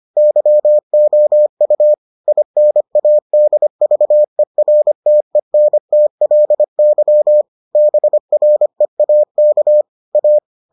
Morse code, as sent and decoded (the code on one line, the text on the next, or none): -.-- --- ..- / .. -. .- -.. ...- . .-. - . -. - .-.. -.-- / -... .-. .- -.- / .-
YOU INADVERTENTLY BREAK A